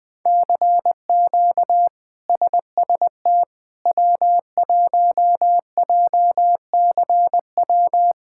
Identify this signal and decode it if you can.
Morse code, heard as a keyed tone, about 21 wpm, 700 Hz. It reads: CQ SST W1JCW
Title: K1USN SST QSO Practice - 20wpm